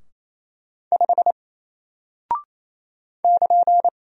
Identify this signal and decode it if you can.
Morse code, heard as a keyed tone, about 27 wpm, 700 Hz. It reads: 5 <KN>
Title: Single Character (Letter:Number:Punctuation:Prosign) Alt 28wpm